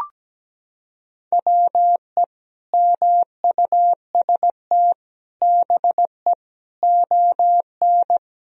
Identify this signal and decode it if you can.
Morse code, heard as a keyed tone, about 17 wpm, 700 Hz. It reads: WE MUST BE ON
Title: Sentences from Top 1000 Words - Review Part 4 - 17wpm